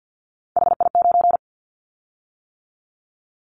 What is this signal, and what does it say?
Morse code, heard as a keyed tone, about 50 wpm, 700 Hz. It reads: HI9
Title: Callsign Prefixes for Top 100 DXCC - 50wpm